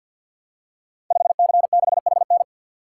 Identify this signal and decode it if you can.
Morse code, heard as a keyed tone, about 49 wpm, 700 Hz. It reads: 5C6HN
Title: Code Groups - 5 Letter:Numbers - 50wpm